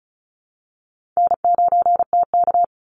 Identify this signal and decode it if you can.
Morse code, heard as a keyed tone, about 34 wpm, 700 Hz. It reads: N9TK
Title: Easy Call Signs Repeated 2x - 35wpm